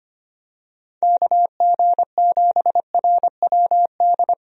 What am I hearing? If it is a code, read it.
Morse code, heard as a keyed tone, about 25 wpm, 700 Hz. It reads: KG7RWD
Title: USA Call Signs - 6 Characters - 25wpm